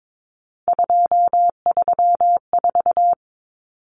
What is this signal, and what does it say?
Morse code, heard as a keyed tone, about 22 wpm, 700 Hz. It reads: 234
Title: Numbers - 3-Digit 22wpm